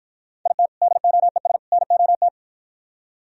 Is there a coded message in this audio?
Yes: Morse code from 0.5 to 2.3 s, heard as a keyed tone, about 53 wpm, 700 Hz. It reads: IT DOES NOT